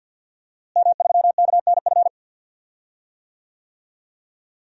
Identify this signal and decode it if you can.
Morse code, heard as a keyed tone, about 47 wpm, 700 Hz. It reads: M3KNF
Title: DXCC-100 Call Signs - 5 Characters - Repeated 2x - 50wpm